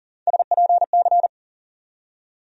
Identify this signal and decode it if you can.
Morse code, heard as a keyed tone, about 40 wpm, 700 Hz. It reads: SPC
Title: USA Call Sign Postfixes - 3 Letters 40wpm